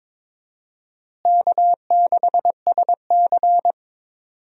Morse code, heard as a keyed tone, about 22 wpm, 700 Hz. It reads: K6SC
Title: USA Call Signs - 4 Characters - Repeated 2x - 22wpm